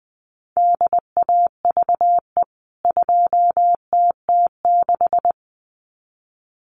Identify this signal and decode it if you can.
Morse code, heard as a keyed tone, about 20 wpm, 700 Hz. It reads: DAVE 2TT6
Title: CWT Exchange 20wpm